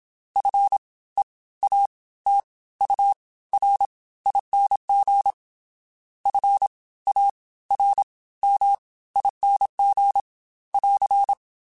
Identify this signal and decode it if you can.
Morse code, heard as a keyed tone, about 27 wpm, spaced as for 17 wpm, 800 Hz. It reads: FEATURING FARMING <AR>